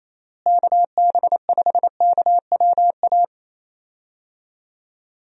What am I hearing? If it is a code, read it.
Morse code, heard as a keyed tone, about 28 wpm, 700 Hz. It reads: KB5KWA
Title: USA Call Signs - 6 Characters - Repeated 3x - 28wpm